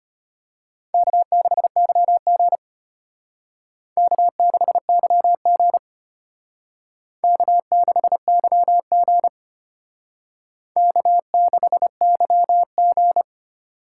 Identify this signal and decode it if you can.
Morse code, heard as a keyed tone, about 32 wpm, 700 Hz. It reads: K6YG K6YG K6YG K6YG